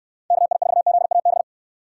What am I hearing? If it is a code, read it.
Morse code, heard as a keyed tone, about 69 wpm, 700 Hz. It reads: KI4ZAB